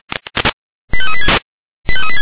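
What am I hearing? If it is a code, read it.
Five-tone selective call (ZVEI), heard as quick runs of five tones, tones of about 70 ms. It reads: 85386
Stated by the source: Five-tone ZVEI selective call 89730 received by SDR.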